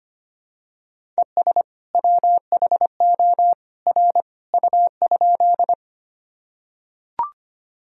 Morse code, heard as a keyed tone, about 25 wpm, 700 Hz. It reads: ES WHO R U?
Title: Sentences from Top 100 Words and Abbreviations 25wpm